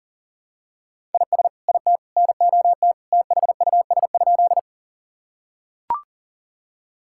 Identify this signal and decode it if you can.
Morse code, heard as a keyed tone, about 40 wpm, 700 Hz. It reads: IS IT NOT THUS?
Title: Sentences from Top 1000 Words - Review Part 5 - 40wpm